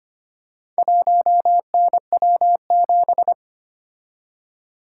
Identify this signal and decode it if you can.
Morse code, heard as a keyed tone, about 25 wpm, 700 Hz. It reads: 1NW7
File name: practice / 25wpm / Four Letter-Number Combinations - 25wpm